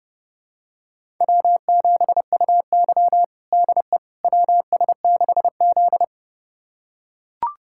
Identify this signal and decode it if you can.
Morse code, heard as a keyed tone, about 32 wpm, 700 Hz. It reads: W7UY DE WS6Z
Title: USA Callsign DE Callsign - 4 Characters - 30wpm